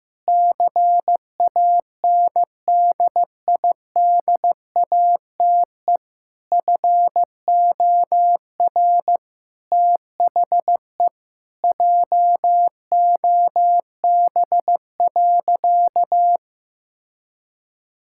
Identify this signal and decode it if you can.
Morse code, heard as a keyed tone, about 15 wpm, 700 Hz. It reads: CANDIDATE FOR THE JOB.
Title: Sentences from Top 1600 Words 15wpm